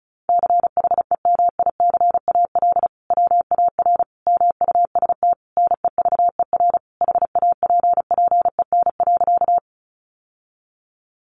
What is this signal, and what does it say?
Morse code, heard as a keyed tone, about 34 wpm, 700 Hz. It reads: CHEMICAL WAR MUST NEVER HAPPEN.
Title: Sentences from Top 1200 Words 35wpm